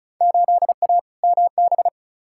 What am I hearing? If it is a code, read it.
Morse code, heard as a keyed tone, about 36 wpm, 690 Hz. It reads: 8A MB
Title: Field Day Exchanges Encore - 35wpm